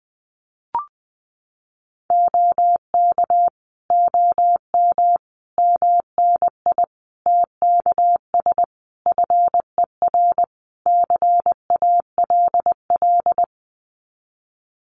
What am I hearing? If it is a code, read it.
Morse code, heard as a keyed tone, about 20 wpm, 700 Hz. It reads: OK OM MNI TKS FER CALL
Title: Common QSO Phrases 20wpm